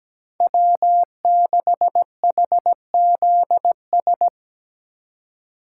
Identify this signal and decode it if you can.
Morse code, heard as a keyed tone, about 17 wpm, 700 Hz. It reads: W6HZS
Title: USA Call Signs - 5 Characters - 2x Character Spacing - 17wpm